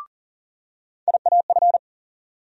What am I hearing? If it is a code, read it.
Morse code, heard as a keyed tone, about 40 wpm, 700 Hz. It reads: IAF
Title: USA Call Sign Postfixes - 3 Letters 40wpm